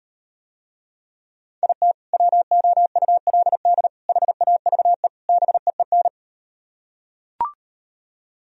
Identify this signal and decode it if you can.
Morse code, heard as a keyed tone, about 38 wpm, 700 Hz. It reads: IT WOULD HAVE BEEN